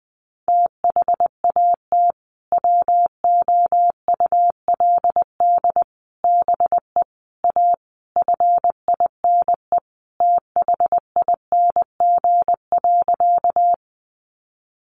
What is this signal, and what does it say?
Morse code, heard as a keyed tone, about 20 wpm, 700 Hz. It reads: THAT WOULD BE A FINE THING.
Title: Sentences from Top 500 Words 20wpm